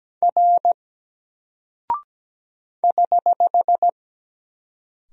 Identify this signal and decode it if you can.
Morse code, heard as a keyed tone, about 17 wpm, 700 Hz. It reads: R <HH>
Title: Single Character (Letter:Number:Punctuation:Prosign) 17wpm